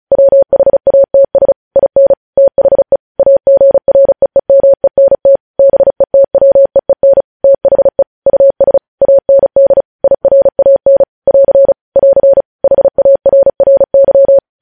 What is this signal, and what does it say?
Morse code, heard as a keyed tone, about 36 wpm, 550 Hz. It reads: WHATS IN THE AGREEMENT BETWEEN THE US AND IRAN <AR> <AR> HARRY
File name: podcast / News headlines-1x-35x35